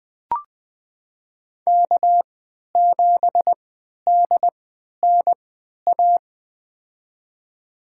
Morse code, heard as a keyed tone, about 20 wpm, spaced as for 11 wpm, 700 Hz. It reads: K7DNA
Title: USA Call Signs - 5 Characters - 3x Character Spacing - 20wpm